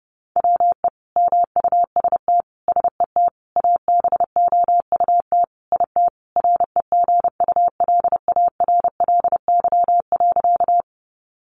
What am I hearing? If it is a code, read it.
Morse code, heard as a keyed tone, about 30 wpm, 700 Hz. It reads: WE MUST SET ABOUT IT REGULARLY.
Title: Sentences from Top 1400 Words 30wpm